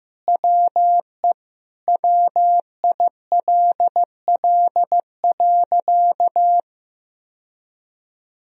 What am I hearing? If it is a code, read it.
Morse code, heard as a keyed tone, about 15 wpm, 700 Hz. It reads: WE WILL.
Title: Sentences from Top 1000 Words - Review Part 1 - 15wpm